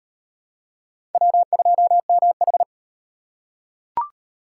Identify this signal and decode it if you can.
Morse code, heard as a keyed tone, about 38 wpm, 700 Hz. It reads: W2MH